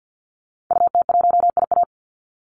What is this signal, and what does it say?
Morse code, heard as a keyed tone, about 50 wpm, 700 Hz. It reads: UT1IA
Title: Call Signs - 50wpm